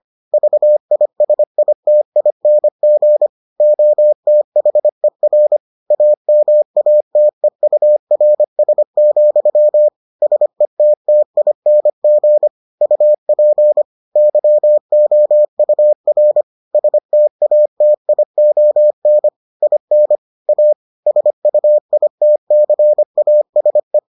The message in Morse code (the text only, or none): VISITING OTHER AMATEURS, SETTING UP YOUR STATION IN A SUITCASE